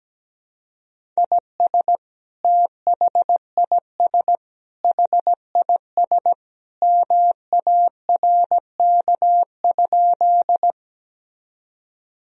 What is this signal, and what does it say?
Morse code, heard as a keyed tone, about 17 wpm, 700 Hz. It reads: IS THIS HIS MARK?
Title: Sentences from Top 1000 Words - Review Part 1 - 17wpm